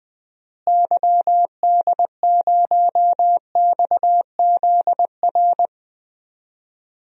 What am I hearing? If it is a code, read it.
Morse code, heard as a keyed tone, about 20 wpm, 700 Hz. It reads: YD0XZR
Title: DXCC-100 Call Signs - 6 Characters - Repeated 2x - 20wpm